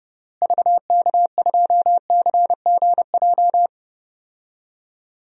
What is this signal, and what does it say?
Morse code, heard as a keyed tone, about 30 wpm, 700 Hz. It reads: VK2CGJ
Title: DXCC-100 Call Signs - 6 Characters - Repeated 3x - 30wpm